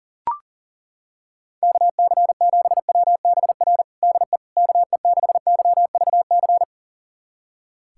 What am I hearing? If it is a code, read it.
Morse code, heard as a keyed tone, about 40 wpm, 700 Hz. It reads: KC7WBR DE KE6YVC
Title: USA Callsign DE Callsign - 6 Characters - 40wpm